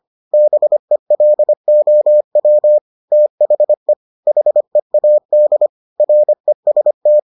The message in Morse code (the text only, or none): BELOW THE HEAD REST